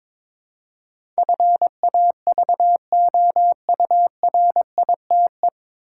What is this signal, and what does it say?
Morse code, heard as a keyed tone, about 23 wpm, 700 Hz. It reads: FAVOURITE
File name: 9 Letter Words - First 3 Letters Given - 22wpm